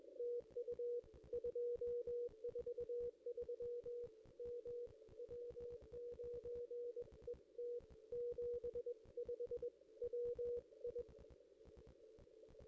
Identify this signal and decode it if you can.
Morse code, heard as a keyed tone, about 21 wpm, 480 Hz. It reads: TU 243 M W9ET 75 WI
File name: contests / ss-cw-k5tr-op-n5rz -1600